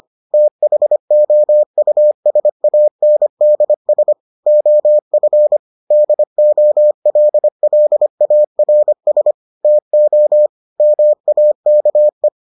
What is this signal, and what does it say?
Morse code, heard as a keyed tone, about 25 wpm, 600 Hz. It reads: THOUSANDS OF DOLLARS TO MAKE